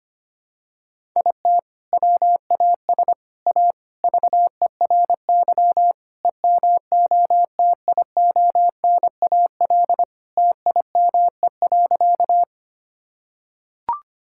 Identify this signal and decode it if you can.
Morse code, heard as a keyed tone, about 25 wpm, 700 Hz. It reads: IT WAS A VERY EMOTIONAL TIME.
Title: Sentences from Top 1000 Words - Review Part 5 - 25wpm